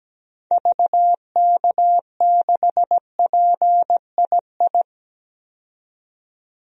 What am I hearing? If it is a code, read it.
Morse code, heard as a keyed tone, about 17 wpm, 700 Hz. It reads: VK6PII
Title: DXCC-100 Call Signs - 6 Characters - 17wpm